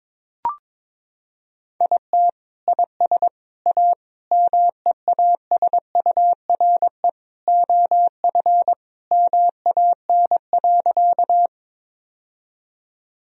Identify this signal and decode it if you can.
Morse code, heard as a keyed tone, about 22 wpm, 700 Hz. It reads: IT IS A MEASURE OF MAN.
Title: Sentences from Top 1000 Words - Review Part 2 - 22wpm